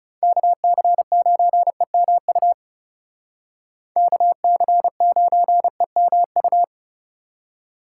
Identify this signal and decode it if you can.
Morse code, heard as a keyed tone, about 38 wpm, 700 Hz. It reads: KC9EMU KC9EMU
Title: Speed Racing - USA 2x3 Call Signs 30wpm